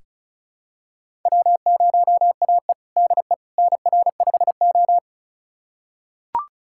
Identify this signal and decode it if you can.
Morse code, heard as a keyed tone, about 36 wpm, 700 Hz. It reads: W0AE DE NR5O
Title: USA Callsign DE Callsign - 4 Characters - 35wpm